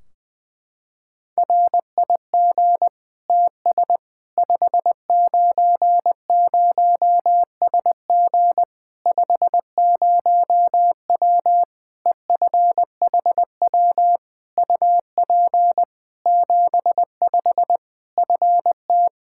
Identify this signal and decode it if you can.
Morse code, heard as a keyed tone, about 20 wpm, 700 Hz. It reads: RIG TS 590SG 50W EFHW UP 75 FT
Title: Practice QSO - Exchange 2 - Info - 20wpm